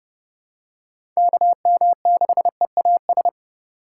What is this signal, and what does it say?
Morse code, heard as a keyed tone, about 30 wpm, 700 Hz. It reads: KM6EAS
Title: USA Call Signs - 6 Characters - 4x Character Spacing - 30wpm